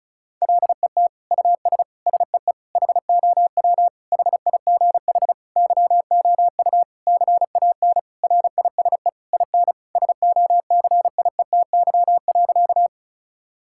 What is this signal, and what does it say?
Morse code, heard as a keyed tone, about 34 wpm, 700 Hz. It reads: LET US SEE HOW HIGH YOU CAN RISE IN SOCIETY.
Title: Sentences from Top 1000 Words - Review Part 6 - 35wpm